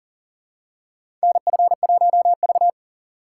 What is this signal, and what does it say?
Morse code, heard as a keyed tone, about 40 wpm, 700 Hz. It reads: NF1V
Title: USA Call Signs - 6x Character Spacing - 40wpm